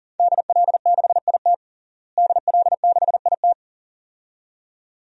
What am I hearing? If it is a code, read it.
Morse code, heard as a keyed tone, about 40 wpm, 700 Hz. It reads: DL6IT DL6IT
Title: DXCC-100 Call Signs - 5 Characters - Repeated 2x - 40wpm